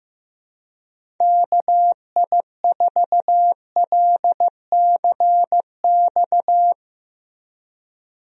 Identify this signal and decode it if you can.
Morse code, heard as a keyed tone, about 15 wpm, 700 Hz. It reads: KI4LCX
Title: USA Call Signs - 6 Characters - Repeated 2x - 15wpm